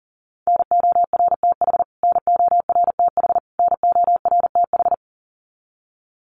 Morse code, heard as a keyed tone, about 40 wpm, 700 Hz. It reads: NORTH NORTH NORTH
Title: Top 300 Words Repeated 40wpm